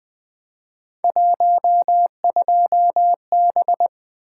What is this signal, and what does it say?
Morse code, heard as a keyed tone, about 20 wpm, 700 Hz. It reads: 12B
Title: Three Letter-Number Combinations - 20wpm